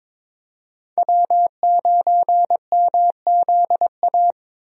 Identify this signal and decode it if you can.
Morse code, heard as a keyed tone, about 22 wpm, 700 Hz. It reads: W9MZA
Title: USA Call Signs - 5 Characters - Repeated 2x - 22wpm